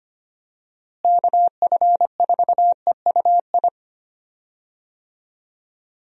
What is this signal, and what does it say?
Morse code, heard as a keyed tone, about 25 wpm, 700 Hz. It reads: KF4EUI
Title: USA Call Signs - 6 Characters - Repeated 2x - 25wpm